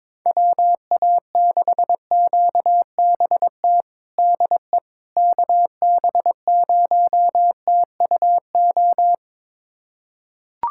Morse code, heard as a keyed tone, about 22 wpm, 700 Hz. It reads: WA6QBT DE KB0TUO
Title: USA Callsign DE Callsign 22wpm